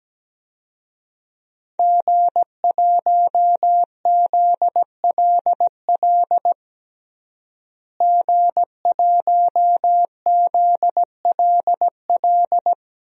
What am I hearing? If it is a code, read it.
Morse code, heard as a keyed tone, about 17 wpm, 700 Hz. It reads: G1ZLL G1ZLL
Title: DXCC-100 Call Signs - 5 Characters - Repeated 3x - 17wpm